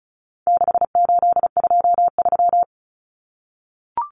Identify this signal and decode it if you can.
Morse code, heard as a keyed tone, about 34 wpm, 700 Hz. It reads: 6823
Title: Numbers - 4-Digit 35wpm